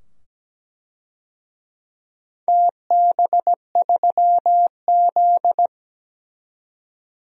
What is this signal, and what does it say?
Morse code, heard as a keyed tone, about 17 wpm, 700 Hz. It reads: TB3Z
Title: DXCC-100 Call Signs - 4 Characters - 17wpm